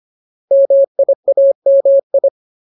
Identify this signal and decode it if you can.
Morse code, heard as a keyed tone, about 25 wpm, 550 Hz. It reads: MIAMI
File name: Ultimate Word List / 25WPM Ultimate